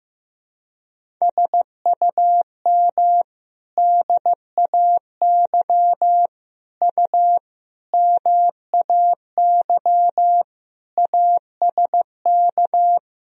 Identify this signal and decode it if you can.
Morse code, heard as a keyed tone, about 15 wpm, 700 Hz. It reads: SUM DAY U MAY ASK
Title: Sentences from Top 200 Words and Abbreviations 15wpm